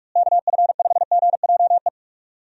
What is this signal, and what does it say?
Morse code, heard as a keyed tone, about 46 wpm, 700 Hz. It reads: KF5GJE
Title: USA Call Signs - 6 Characters - Repeated 3x - 45wpm